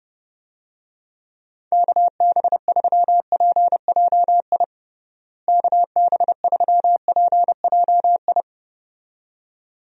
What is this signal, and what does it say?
Morse code, heard as a keyed tone, about 30 wpm, 700 Hz. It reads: KB3PJI KB3PJI
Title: USA Call Signs - 6 Characters - Repeated 2x - 30wpm